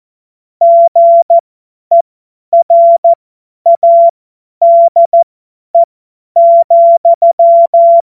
Morse code, heard as a keyed tone, about 14 wpm, spaced as for 10 wpm, 680 Hz. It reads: GERADE,
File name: Morsekurs / Witze / Witz 21. Juni 2022 50 F70